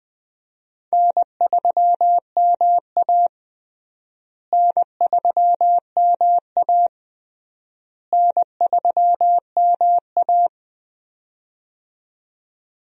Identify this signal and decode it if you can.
Morse code, heard as a keyed tone, about 20 wpm, 700 Hz. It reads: N3MA N3MA N3MA
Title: USA Call Signs - 4 Characters - Repeated 3x - 20wpm